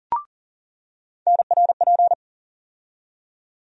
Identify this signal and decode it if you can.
Morse code, heard as a keyed tone, about 40 wpm, 700 Hz. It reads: NRP
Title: USA Call Sign Postfixes - 3 Letters 40wpm